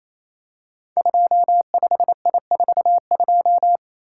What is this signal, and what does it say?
Morse code, heard as a keyed tone, about 29 wpm, 700 Hz. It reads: 25I42
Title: Code Groups - 5 Letter:Numbers - 28wpm